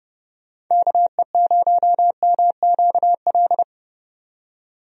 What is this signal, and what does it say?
Morse code, heard as a keyed tone, about 31 wpm, 700 Hz. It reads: KE0MQL
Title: USA Call Signs - 6 Characters - 30wpm